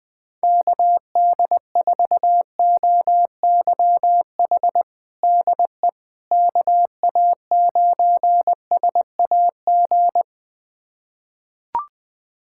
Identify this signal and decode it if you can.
Morse code, heard as a keyed tone, about 20 wpm, 700 Hz. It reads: KD4OYH DE KA9SAG
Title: USA Callsign DE Callsign - 6 Characters - 20wpm